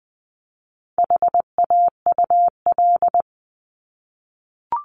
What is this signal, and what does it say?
Morse code, heard as a keyed tone, about 20 wpm, 700 Hz. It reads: HAUL
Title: 260 Lesson 42e - Words 20-20wpm